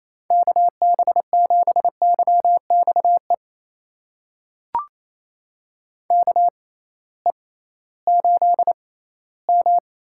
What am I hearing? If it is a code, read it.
Morse code, heard as a keyed tone, about 28 wpm, spaced as for 16 wpm, 700 Hz. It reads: KB7YXE KE8M
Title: USA Call Signs - 6 Characters - 6x Character Spacing - 28wpm